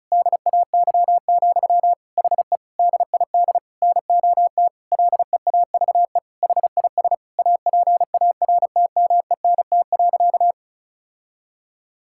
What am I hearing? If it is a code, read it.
Morse code, heard as a keyed tone, about 36 wpm, 700 Hz. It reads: DAY, HE DID NOT LEAVE HIS APARTMENT.
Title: Sentences from Top 1000 Words - Review Part 6 - 35wpm